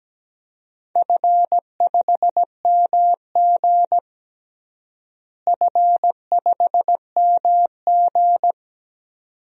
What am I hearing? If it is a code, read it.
Morse code, heard as a keyed tone, about 17 wpm, 700 Hz. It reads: F5MG F5MG
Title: DXCC-100 Call Signs - 4 Characters - Repeated 2x - 17wpm